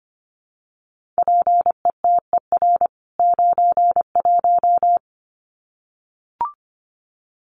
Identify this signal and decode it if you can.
Morse code, heard as a keyed tone, about 25 wpm, 700 Hz. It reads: PETER 91
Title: CWT Exchange 25wpm